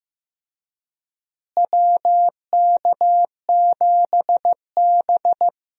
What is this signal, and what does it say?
Morse code, heard as a keyed tone, about 15 wpm, 700 Hz. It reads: WK7B
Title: USA Call Signs - 4 Characters - Repeated 3x - 15wpm